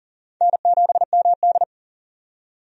Morse code, heard as a keyed tone, about 40 wpm, 700 Hz. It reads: N7MD
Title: USA Call Signs - 4 Characters - 40wpm